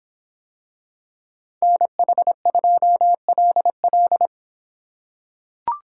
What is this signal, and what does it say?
Morse code, heard as a keyed tone, about 24 wpm, 700 Hz. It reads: NH2LL